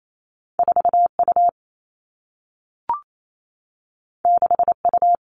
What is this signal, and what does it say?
Morse code, heard as a keyed tone, about 28 wpm, 700 Hz. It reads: 4 U 6 U
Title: Two Challenging Letter-Number Combinations 28wpm